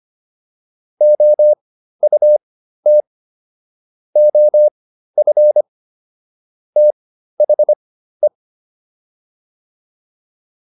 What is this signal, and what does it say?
Morse code, heard as a keyed tone, about 25 wpm, spaced as for 13 wpm, 600 Hz. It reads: OUT OF THE